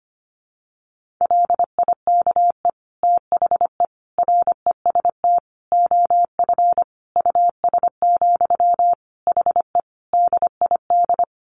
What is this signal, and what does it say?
Morse code, heard as a keyed tone, about 25 wpm, 700 Hz. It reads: LIKE THE REST OF US, HE DID
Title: Sentences from Top 1400 Words 25wpm